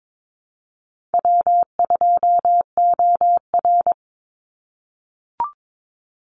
Morse code, heard as a keyed tone, about 22 wpm, 700 Hz. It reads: W2OR
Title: Easy Call Signs - 22wpm